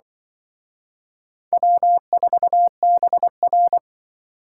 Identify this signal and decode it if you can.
Morse code, heard as a keyed tone, about 25 wpm, 700 Hz. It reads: W4BR
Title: Speed Racing - USA 1x2 Call Signs 20wpm